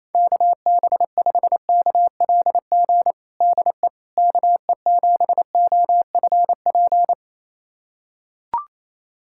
Morse code, heard as a keyed tone, about 28 wpm, 700 Hz. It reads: KB5KLG DE KE7OFP
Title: USA Callsign DE Callsign 28wpm